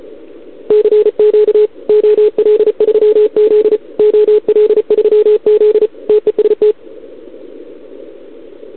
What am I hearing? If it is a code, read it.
Morse code, heard as a keyed tone, about 35 wpm, 440 Hz. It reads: CQ OL3Z OL3Z TEST
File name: CQWWCW2021 / OL3Z